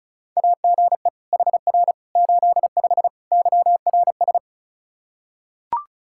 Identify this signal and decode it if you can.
Morse code, heard as a keyed tone, about 34 wpm, 700 Hz. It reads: AGE HR 85 YRS
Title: Practice QSO - Exchange 3 - More Info - 35wpm